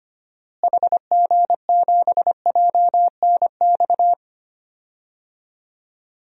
Morse code, heard as a keyed tone, about 25 wpm, 700 Hz. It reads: HG7JNX
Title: DXCC-100 Call Signs - 6 Characters - Repeated 3x - 25wpm